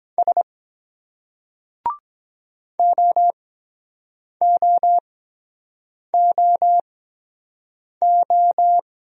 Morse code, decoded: S OOOO

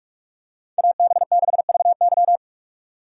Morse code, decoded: AB6VY

